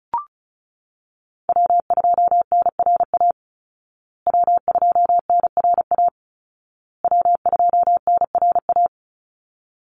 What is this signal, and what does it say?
Morse code, heard as a keyed tone, about 34 wpm, 700 Hz. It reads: W2NRA W2NRA W2NRA
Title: Call Signs Repeated 2x - 35wpm